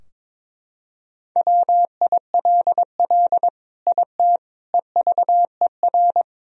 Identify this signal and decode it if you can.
Morse code, heard as a keyed tone, about 22 wpm, 700 Hz. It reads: WILL IT EVER
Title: Sentences from Top 1000 Words - Review Part 2 - 22wpm